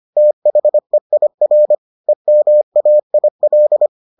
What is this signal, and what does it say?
Morse code, heard as a keyed tone, about 25 wpm, 600 Hz. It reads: THEIR EMAIL